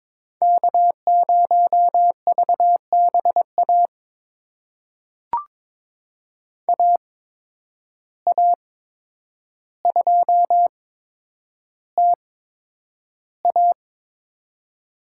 Morse code, heard as a keyed tone, about 22 wpm, spaced as for 9 wpm, 700 Hz. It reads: K0VBA AA2TA
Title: USA Call Signs - 5 Characters - 8x Character Spacing - 22wpm